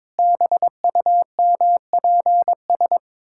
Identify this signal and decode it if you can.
Morse code, heard as a keyed tone, about 21 wpm, 700 Hz. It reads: BUMPS
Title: Pangram Challenge v2 - 22wpm